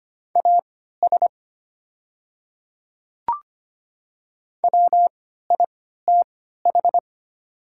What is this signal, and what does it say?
Morse code, heard as a keyed tone, about 25 wpm, spaced as for 14 wpm, 700 Hz. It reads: AS WITH
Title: 3x Character Spacing - Top 100 Words - 25wpm